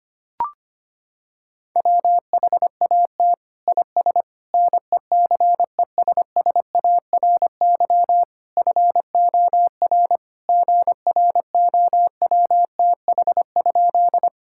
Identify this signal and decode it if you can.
Morse code, heard as a keyed tone, about 25 wpm, 700 Hz. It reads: WHAT IS NECESSARY FOR GROWTH?
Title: Sentences from Top 1000 Words - Review Part 5 - 25wpm